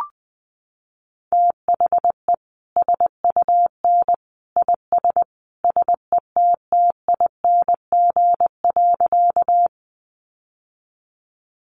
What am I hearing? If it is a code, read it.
Morse code, heard as a keyed tone, about 20 wpm, 700 Hz. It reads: THE SUN IS SETTING.
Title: Sentences from Top 1300 Words 20wpm